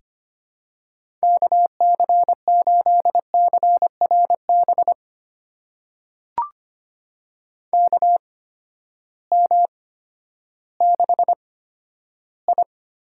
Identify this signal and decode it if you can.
Morse code, heard as a keyed tone, about 25 wpm, spaced as for 12 wpm, 700 Hz. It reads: KC8CRB KM6I
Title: USA Call Signs - 8x Character Spacing - 25wpm